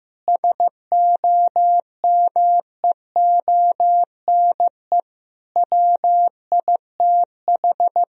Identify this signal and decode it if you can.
Morse code, heard as a keyed tone, about 15 wpm, 700 Hz. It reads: SOMEONE WITH
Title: Sentences from Top 1000 Words - Review Part 4 - 15wpm